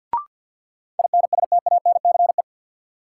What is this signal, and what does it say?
Morse code, heard as a keyed tone, about 47 wpm, 700 Hz. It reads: INSTANCE
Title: Sentences from Top 800 Words - New Word Drill - 50wpm